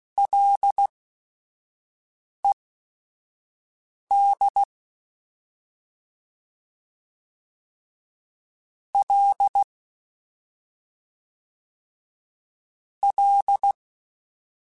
Morse code, heard as a keyed tone, about 16 wpm, spaced as for 5 wpm, 800 Hz. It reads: LED L L
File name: Ls / Single L m3s / Single L @5